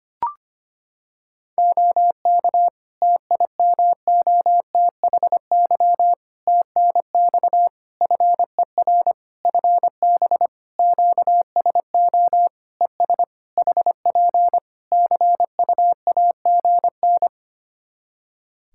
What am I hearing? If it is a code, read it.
Morse code, heard as a keyed tone, about 25 wpm, 700 Hz. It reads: OK TIMOTHY TNX FER FB QSO ES HP CUAGN
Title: K1USN SST QSO Practice 25wpm